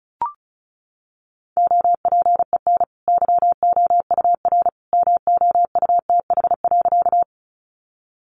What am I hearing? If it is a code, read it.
Morse code, heard as a keyed tone, about 34 wpm, 700 Hz. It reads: OPEN YOUR MOUTH.
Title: Sentences from Top 1500 Words 35wpm